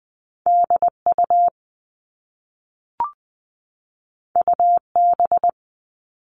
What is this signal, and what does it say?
Morse code, heard as a keyed tone, about 20 wpm, 700 Hz. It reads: DU UB
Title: All Cut Numbers - 2-Digit 20wpm